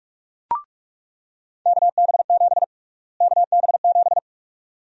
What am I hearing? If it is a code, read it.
Morse code, heard as a keyed tone, about 45 wpm, 700 Hz. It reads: KB7 KB7
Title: USA Call Sign Prefixes - Repeated 2x - 45wpm